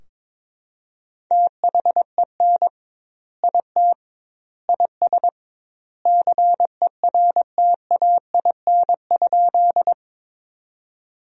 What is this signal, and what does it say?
Morse code, heard as a keyed tone, about 22 wpm, 700 Hz. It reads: THEN IT IS CERTAIN?
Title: 2x Word-Spacing - Sentences from Top 500 Words - 22wpm